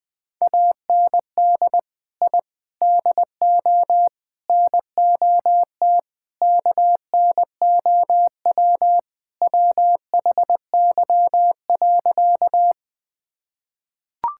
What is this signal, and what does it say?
Morse code, heard as a keyed tone, about 20 wpm, 700 Hz. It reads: AND I DO NOT KNOW WHY.
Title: Sentences with 5-Word Phrases - Spoken Beginning - 20wpm